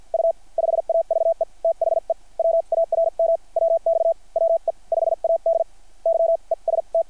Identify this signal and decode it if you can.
Morse code, heard as a keyed tone, about 50 wpm, spaced as for 42 wpm, 660 Hz. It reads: U 5AVE THE WARM WX WE 5AD YEST
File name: archived / CRSnet